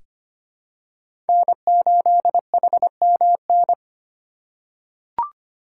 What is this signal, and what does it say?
Morse code, heard as a keyed tone, about 25 wpm, 700 Hz. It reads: N8HMN
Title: USA Call Signs - 8x Character Spacing - 25wpm